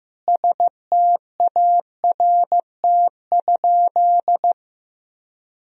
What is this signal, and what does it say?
Morse code, heard as a keyed tone, about 15 wpm, 700 Hz. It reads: START?
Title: Sentences from Top 1000 Words - Review Part 1 - 15wpm